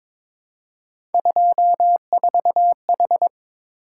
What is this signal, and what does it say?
Morse code, heard as a keyed tone, about 22 wpm, 700 Hz. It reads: 24H
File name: practice / 22wpm / Three Letter-Number Combinations - 22wpm